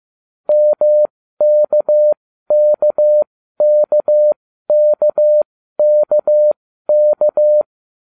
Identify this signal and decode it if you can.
Morse code, heard as a keyed tone, about 15 wpm, spaced as for 12 wpm, 600 Hz. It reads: MKKKKKK